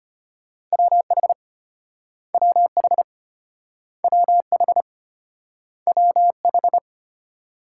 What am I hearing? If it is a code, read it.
Morse code, heard as a keyed tone, about 34 wpm, 700 Hz. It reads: WH WH WH WH